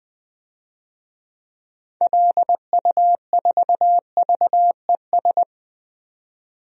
Morse code, heard as a keyed tone, about 20 wpm, 700 Hz. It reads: LU4VES
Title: DXCC-100 Call Signs - 6 Characters - 20wpm